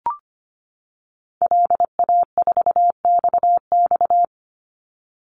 Morse code, heard as a keyed tone, about 25 wpm, 700 Hz. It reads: LA4XX
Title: Call Signs - 25wpm